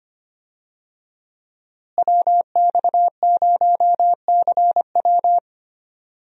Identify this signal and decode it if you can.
Morse code, heard as a keyed tone, about 25 wpm, 700 Hz. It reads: WX0CW
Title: USA Call Signs - 5 Characters - Repeated 3x - 25wpm